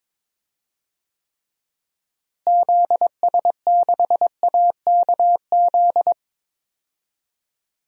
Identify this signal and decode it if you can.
Morse code, heard as a keyed tone, about 22 wpm, 700 Hz. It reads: ZS6AKZ
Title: DXCC-100 Call Signs - 6 Characters - Repeated 3x - 22wpm